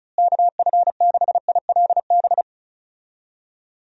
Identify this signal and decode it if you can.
Morse code, heard as a keyed tone, about 34 wpm, 700 Hz. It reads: KF6ILB
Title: USA Call Signs - 6 Characters - Repeated 2x - 35wpm